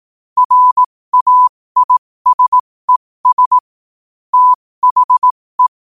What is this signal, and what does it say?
Morse code, heard as a keyed tone, about 18 wpm, spaced as for 15 wpm, 1000 Hz. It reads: RAISES THE
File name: qotdfiles / 15wpm